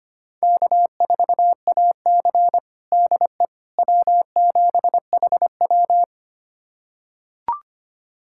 Morse code, heard as a keyed tone, about 25 wpm, 700 Hz. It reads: K4AC DE W7HW
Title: USA Callsign DE Callsign - 4 Characters - 25wpm